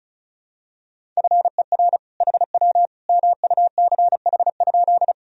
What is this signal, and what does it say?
Morse code, heard as a keyed tone, about 34 wpm, 700 Hz. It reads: FER HW MUCH?
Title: Sentences from Top 200 Words and Abbreviations 35wpm